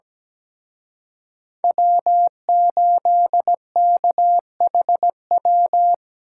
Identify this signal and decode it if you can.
Morse code, heard as a keyed tone, about 20 wpm, 700 Hz. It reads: W8KHW
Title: Speed Racing - USA Call Signs 17wpm